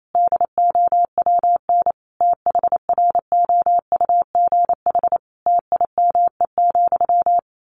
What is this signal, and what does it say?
Morse code, heard as a keyed tone, about 28 wpm, 700 Hz. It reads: DOWN THROUGH TIME,
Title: A Sound Of Thunder 28wpm